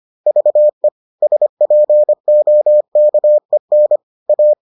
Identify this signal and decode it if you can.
Morse code, heard as a keyed tone, about 25 wpm, 600 Hz. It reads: VE SPOKEN A